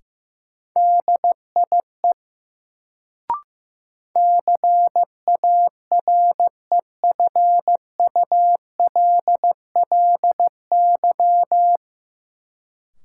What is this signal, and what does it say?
Morse code, heard as a keyed tone, about 15 wpm, 700 Hz. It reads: DIE CAREFULLY
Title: Sentences from Top 1300 Words - New Word Drill - 15wpm